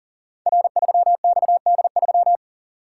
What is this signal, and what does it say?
Morse code, heard as a keyed tone, about 38 wpm, 700 Hz. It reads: R3XD3
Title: Code Groups - 5 Letter:Numbers - 40wpm